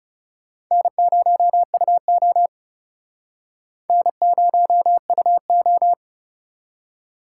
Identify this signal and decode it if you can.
Morse code, heard as a keyed tone, about 38 wpm, 700 Hz. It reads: N0UO N0UO